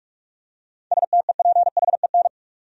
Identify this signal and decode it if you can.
Morse code, heard as a keyed tone, about 47 wpm, 700 Hz. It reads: STEPHEN
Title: USA Common Names - 45wpm